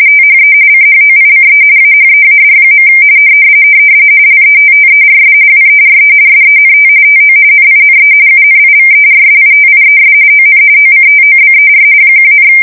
RTTY, about 45.45 baud. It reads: THE CHOICE AHEAD. LETTERS WERE SENT TO NINE UNIVERSITIES ON OCTOBER 1,
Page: 170 Shift 2125/2295